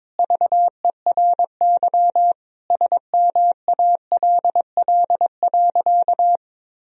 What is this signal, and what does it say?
Morse code, heard as a keyed tone, about 22 wpm, 700 Hz. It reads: VERY SMALL.
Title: Sentences from Top 1000 Words - Review Part 1 - 22wpm